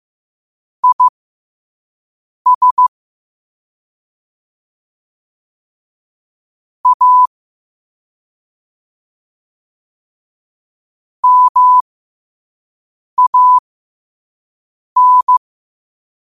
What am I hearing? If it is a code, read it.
Morse code, heard as a keyed tone, about 15 wpm, spaced as for 5 wpm, 1000 Hz. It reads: IS A MAN
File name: qotdfiles / 5wpm